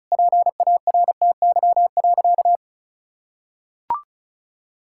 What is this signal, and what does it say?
Morse code, heard as a keyed tone, about 34 wpm, 700 Hz. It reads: PARTY.
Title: Sentences from Top 1600 Words 35wpm